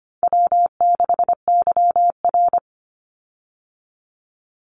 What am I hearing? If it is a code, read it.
Morse code, heard as a keyed tone, about 25 wpm, 700 Hz. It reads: W6YR
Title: Easy Call Signs - 25wpm